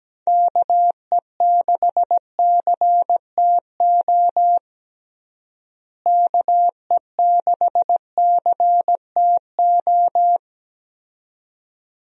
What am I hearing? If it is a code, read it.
Morse code, heard as a keyed tone, about 17 wpm, 700 Hz. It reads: KE6CTO KE6CTO
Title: USA Call Signs - 6 Characters - Repeated 2x - 17wpm